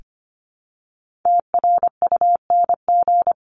Morse code, heard as a keyed tone, about 25 wpm, 700 Hz. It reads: TRUNG
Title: CWT Exchange 25wpm